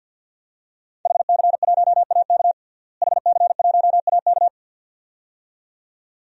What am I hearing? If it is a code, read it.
Morse code, heard as a keyed tone, about 46 wpm, 700 Hz. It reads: HC1AK HC1AK
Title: DXCC-100 Call Signs - 5 Characters - Repeated 2x - 50wpm